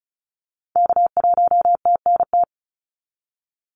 Morse code, heard as a keyed tone, about 36 wpm, 700 Hz. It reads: K1TNT
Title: NA SOTA Activator Callsigns 35wpm